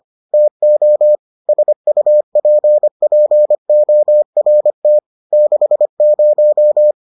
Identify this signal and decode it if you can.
Morse code, heard as a keyed tone, about 25 wpm, 600 Hz. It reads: TO SUPPORT 60